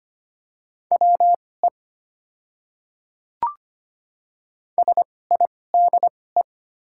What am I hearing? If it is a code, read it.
Morse code, heard as a keyed tone, about 25 wpm, spaced as for 18 wpm, 700 Hz. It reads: WE SIDE
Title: 2x Character Spacing - Top 200 Words - 25wpm